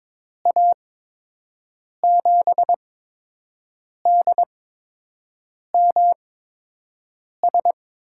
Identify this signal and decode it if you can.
Morse code, heard as a keyed tone, about 22 wpm, spaced as for 6 wpm, 700 Hz. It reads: A7DMS
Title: USA Call Signs - 6 Characters - 8x Character Spacing - 22wpm